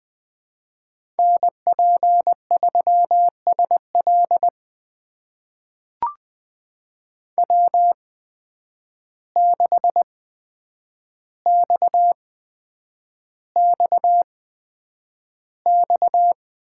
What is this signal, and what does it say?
Morse code, heard as a keyed tone, about 20 wpm, spaced as for 9 wpm, 700 Hz. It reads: NP3SL W6XXX
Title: USA Call Signs - 5 Characters - 8x Character Spacing - 20wpm